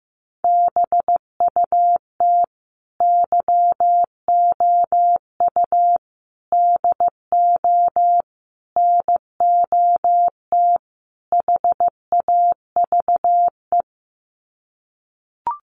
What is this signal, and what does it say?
Morse code, heard as a keyed tone, about 15 wpm, 700 Hz. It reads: BUT YOU DO NOT HAVE
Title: Sets of 5 Words Encore 15wpm